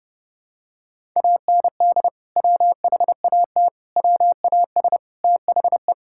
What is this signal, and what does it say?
Morse code, heard as a keyed tone, about 30 wpm, 700 Hz. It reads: AND WHAT WAS THE
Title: Sentences from Top 1000 Words - Review Part 5 - 30wpm